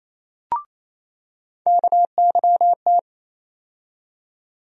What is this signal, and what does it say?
Morse code, heard as a keyed tone, about 28 wpm, 700 Hz. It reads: KYT
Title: USA Call Sign Postfixes - 3 Letters 28wpm